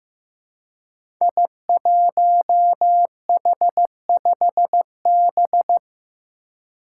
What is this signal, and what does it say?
Morse code, heard as a keyed tone, about 15 wpm, 700 Hz. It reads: I1H5B
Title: Five Challenging Letter-Number Combinations 15wpm